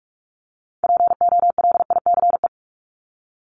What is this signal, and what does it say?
Morse code, heard as a keyed tone, about 44 wpm, 700 Hz. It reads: POLICE
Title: Top 1751-2000 Words 45wpm